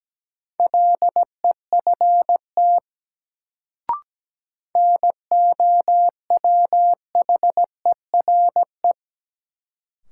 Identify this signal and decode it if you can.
Morse code, heard as a keyed tone, about 17 wpm, 700 Hz. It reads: LEFT NOWHERE
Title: Sets of 5 Words Encore - New Word Drill - 17wpm